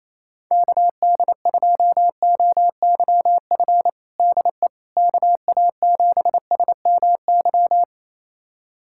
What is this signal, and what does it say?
Morse code, heard as a keyed tone, about 28 wpm, 700 Hz. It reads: KD2OYF DE KA7SMY
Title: USA Callsign DE Callsign - 6 Characters - 28wpm